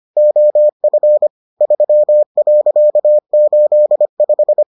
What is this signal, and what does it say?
Morse code, heard as a keyed tone, about 25 wpm, 600 Hz. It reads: OF 3.85